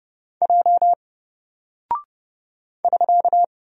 Morse code, heard as a keyed tone, about 31 wpm, 700 Hz. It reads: J <SK>
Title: Single Character (Letter:Number:Punctuation:Prosign) Alt 30wpm